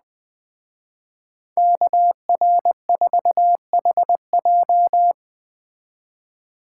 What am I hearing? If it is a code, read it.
Morse code, heard as a keyed tone, about 21 wpm, 700 Hz. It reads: KR4HJ
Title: Speed Racing - USA 2x2 Call Signs 17wpm